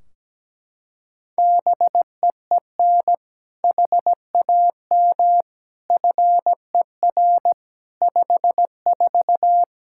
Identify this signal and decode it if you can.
Morse code, heard as a keyed tone, about 17 wpm, 700 Hz. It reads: BEEN HAM FER 54
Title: K1USN SST QSO Practice 17wpm